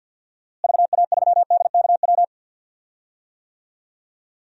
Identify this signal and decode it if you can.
Morse code, heard as a keyed tone, about 50 wpm, 700 Hz. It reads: VA3DKW
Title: DXCC-100 Call Signs - 6 Characters - Repeated 3x - 50wpm